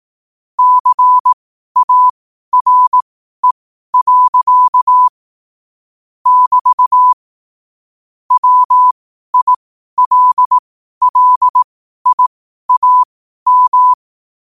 Morse code, heard as a keyed tone, about 18 wpm, spaced as for 12 wpm, 1000 Hz. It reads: CARE. <BT> WILLIAM
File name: qotdfiles / 12wpm